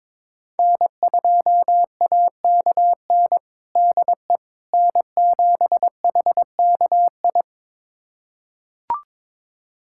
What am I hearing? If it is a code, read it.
Morse code, heard as a keyed tone, about 22 wpm, 700 Hz. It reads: N2AKN DE N7HKI
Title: USA Callsign DE Callsign - 5 Characters - 22wpm